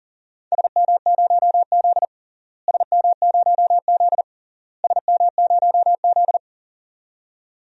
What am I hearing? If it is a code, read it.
Morse code, heard as a keyed tone, about 40 wpm, 700 Hz. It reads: SM0Z SM0Z SM0Z
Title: DXCC-100 Call Signs - 4 Characters - Repeated 3x - 40wpm